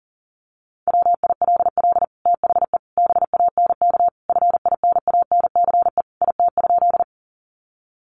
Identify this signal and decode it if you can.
Morse code, heard as a keyed tone, about 40 wpm, 700 Hz. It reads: WILL THE BANK FINANCE IT?
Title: Sentences from Top 1500 Words 40wpm